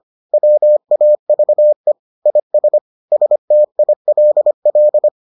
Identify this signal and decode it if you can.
Morse code, heard as a keyed tone, about 25 wpm, 600 Hz. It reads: WAVE IS STILL